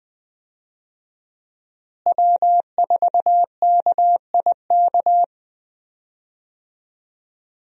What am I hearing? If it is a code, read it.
Morse code, heard as a keyed tone, about 20 wpm, 700 Hz. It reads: W4KIK
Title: USA Call Signs - 5 Characters - Repeated 2x - 20wpm